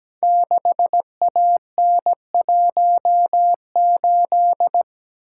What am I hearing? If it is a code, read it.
Morse code, heard as a keyed tone, about 17 wpm, 700 Hz. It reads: 6AN18
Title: Code Groups - 5 Letter:Numbers - 17wpm